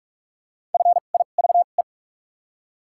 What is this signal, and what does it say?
Morse code, heard as a keyed tone, about 47 wpm, spaced as for 33 wpm, 700 Hz. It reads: FIVE
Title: 2x Character Spacing - Top 500 Words - 45wpm